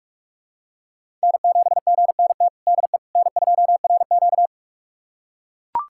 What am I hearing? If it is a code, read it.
Morse code, heard as a keyed tone, about 44 wpm, 700 Hz. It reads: N7GNT DE N2RQ
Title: USA Callsign DE Callsign 45wpm